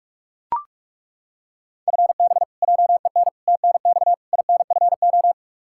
Morse code, heard as a keyed tone, about 41 wpm, 700 Hz. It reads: FB JEN TNX INFO
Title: Practice QSO - Exchange 3 - More Info - 45wpm